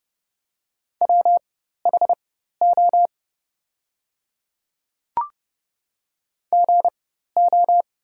Morse code, heard as a keyed tone, about 28 wpm, spaced as for 14 wpm, 700 Hz. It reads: WHO GO
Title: 4x Character Spacing - Top 100 Words - 30wpm